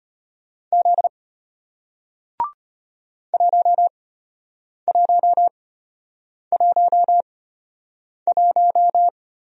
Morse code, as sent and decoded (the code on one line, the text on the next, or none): --.. / .---- .---- .---- .----
Z 1111